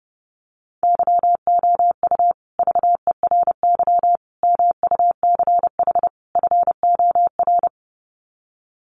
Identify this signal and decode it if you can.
Morse code, heard as a keyed tone, about 30 wpm, 700 Hz. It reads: YOU VERY MUCH FOR
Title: Sets of 4 Words 30wpm